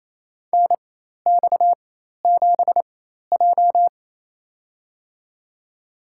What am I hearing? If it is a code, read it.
Morse code, heard as a keyed tone, about 27 wpm, spaced as for 13 wpm, 700 Hz. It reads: NX7J
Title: USA Call Signs - 4 Characters - 4x Character Spacing - 28wpm